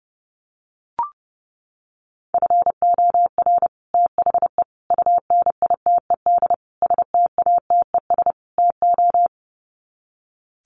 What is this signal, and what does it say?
Morse code, heard as a keyed tone, about 30 wpm, 700 Hz. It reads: FOR THE UNITED STATES TO
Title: Sets of 5 Words Encore 30wpm